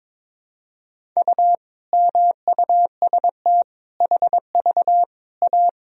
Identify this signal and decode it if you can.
Morse code, heard as a keyed tone, about 22 wpm, 700 Hz. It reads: U MUST HV A
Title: Sentences from Top 500 Words and Abbreviations 22wpm